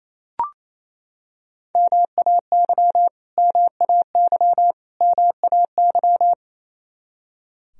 Morse code, heard as a keyed tone, about 28 wpm, 700 Hz. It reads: MAY MAY MAY
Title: Top 100 Words Repeated 28wpm